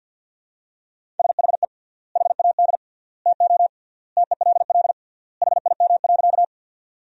Morse code, heard as a keyed tone, about 50 wpm, 700 Hz. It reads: SHE HAD TO TELL HIM.
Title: 3x Word-Spacing - Sentences from Top 200 Words - 50wpm